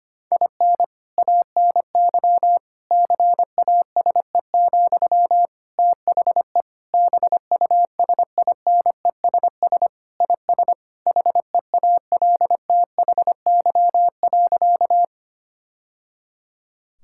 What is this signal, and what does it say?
Morse code, heard as a keyed tone, about 25 wpm, 700 Hz. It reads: IN ANY CASE, THE BUSINESS IS HEALTHY.
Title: Sentences from Top 1000 Words - Review Part 5 - 25wpm